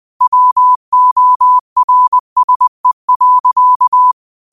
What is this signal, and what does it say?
Morse code, heard as a keyed tone, about 20 wpm, 1000 Hz. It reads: WORSE.
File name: qotdfiles / 20wpm